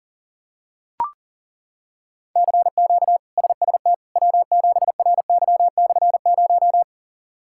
Morse code, heard as a keyed tone, about 40 wpm, 700 Hz. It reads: CQ SST W7RY/0
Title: K1USN SST QSO Practice - 40wpm